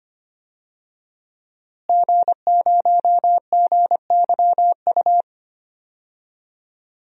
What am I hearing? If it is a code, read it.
Morse code, heard as a keyed tone, about 25 wpm, 700 Hz. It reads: G0GYU
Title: DXCC-100 Call Signs - 5 Characters - Repeated 3x - 25wpm